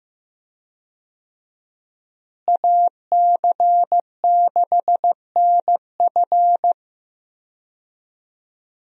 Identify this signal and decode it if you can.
Morse code, heard as a keyed tone, about 16 wpm, 700 Hz. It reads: AC6NF
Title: USA Call Signs - 5 Characters - Repeated 2x - 15wpm